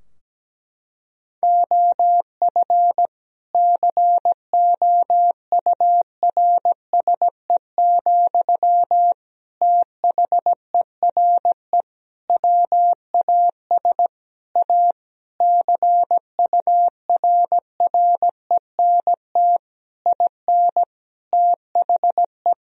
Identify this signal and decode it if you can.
Morse code, heard as a keyed tone, about 17 wpm, 700 Hz. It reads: OF COURSE, THERE WAS A CURRENT IN THE
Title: Sentences from Top 1000 Words - Review Part 4 - 17wpm